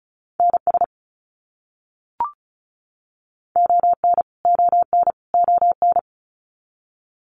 Morse code, heard as a keyed tone, about 34 wpm, 700 Hz. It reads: NS ON ON ON
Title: Canadian Provinces and Territories - Repeated 3x - 35wpm